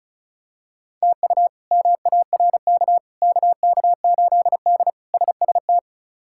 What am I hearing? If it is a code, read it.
Morse code, heard as a keyed tone, about 34 wpm, 700 Hz. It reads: TU MARK KK8D SST
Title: K1USN SST QSO Practice - 35wpm